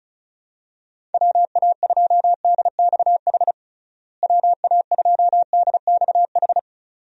Morse code, heard as a keyed tone, about 36 wpm, 700 Hz. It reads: WA2DXH WA2DXH
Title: USA Call Signs - 6 Characters - Repeated 2x - 35wpm